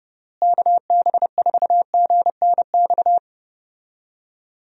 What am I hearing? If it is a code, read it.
Morse code, heard as a keyed tone, about 30 wpm, 700 Hz. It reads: KB4GNX
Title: USA Call Signs - 6 Characters - Repeated 2x - 30wpm